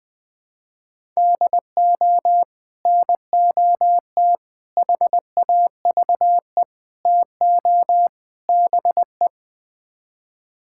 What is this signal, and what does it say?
Morse code, heard as a keyed tone, about 21 wpm, 700 Hz. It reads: DO NOT HAVE TO BE
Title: Build A Phrase - 5-Word Phrases - 20wpm